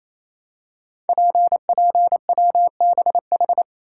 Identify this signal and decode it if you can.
Morse code, heard as a keyed tone, about 27 wpm, 700 Hz. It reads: PPWBH
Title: Code Groups - 5 Letters - 28wpm